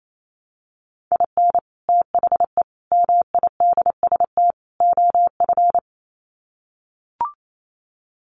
Morse code, heard as a keyed tone, about 28 wpm, 700 Hz. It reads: IN THE MIDST OF
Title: Sets of 4 Words 28wpm